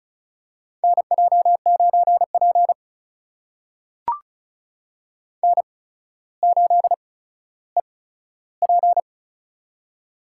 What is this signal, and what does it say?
Morse code, heard as a keyed tone, about 36 wpm, spaced as for 15 wpm, 700 Hz. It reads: NJ9P N8EP
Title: USA Call Signs - 4 Characters - 8x Character Spacing - 35wpm